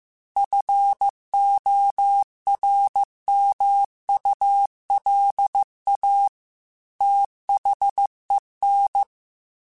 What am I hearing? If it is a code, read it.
Morse code, heard as a keyed tone, about 15 wpm, 800 Hz. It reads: FORMULA THEN